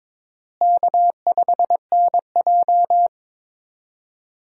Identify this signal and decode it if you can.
Morse code, heard as a keyed tone, about 22 wpm, 700 Hz. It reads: K5NJ
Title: USA Call Signs - 4 Characters - Repeated 3x - 22wpm